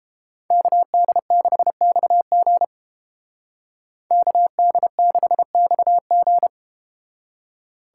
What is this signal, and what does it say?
Morse code, heard as a keyed tone, about 33 wpm, 700 Hz. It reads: KD6XG KD6XG